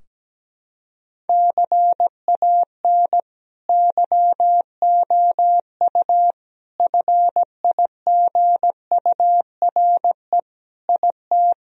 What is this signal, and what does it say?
Morse code, heard as a keyed tone, about 17 wpm, 700 Hz. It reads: CAN YOU FIGURE IT
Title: Sentences from Top 1000 Words - Review Part 2 - 17wpm